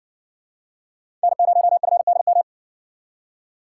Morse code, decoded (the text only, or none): N0FDM